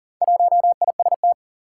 Morse code, heard as a keyed tone, about 42 wpm, 700 Hz. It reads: 1IST
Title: Four Challenging Letter-Number Combinations 40wpm